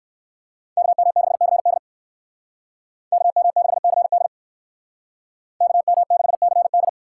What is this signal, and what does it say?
Morse code, heard as a keyed tone, about 58 wpm, 700 Hz. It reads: KM6CD KM6CD KM6CD